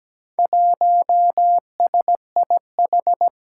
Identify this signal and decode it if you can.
Morse code, heard as a keyed tone, about 17 wpm, 700 Hz. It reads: 1SIH
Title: Four Challenging Letter-Number Combinations 17wpm